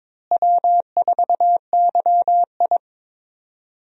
Morse code, heard as a keyed tone, about 22 wpm, 700 Hz. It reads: W4YI
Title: USA Call Signs - 4 Characters - Repeated 2x - 22wpm